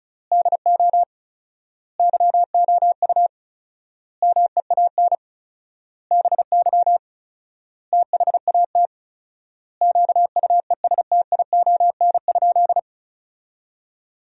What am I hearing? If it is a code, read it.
Morse code, heard as a keyed tone, about 36 wpm, 700 Hz. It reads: DO YOU MEAN BY THAT QUESTION?
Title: 4x Word-Spacing - Sentences from Top 400 Words - 35wpm